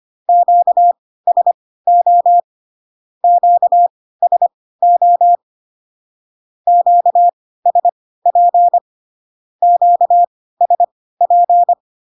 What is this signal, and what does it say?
Morse code, heard as a keyed tone, about 25 wpm, spaced as for 16 wpm, 700 Hz. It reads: QSO QSO QSP QSP
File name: cw / Q-codes most common repeated 3 times 10 25wpm ik1hge